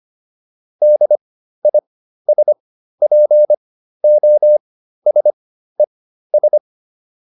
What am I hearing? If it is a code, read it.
Morse code, heard as a keyed tone, about 25 wpm, spaced as for 13 wpm, 600 Hz. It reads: DISPOSES